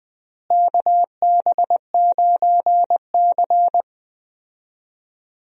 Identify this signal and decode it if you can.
Morse code, heard as a keyed tone, about 20 wpm, 700 Hz. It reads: KB9C
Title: USA Call Signs - 4 Characters - Repeated 3x - 20wpm